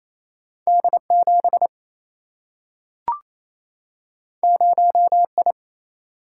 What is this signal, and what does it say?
Morse code, heard as a keyed tone, about 28 wpm, 700 Hz. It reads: D 7 0 I
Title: Single-Letter Single-Number 28wpm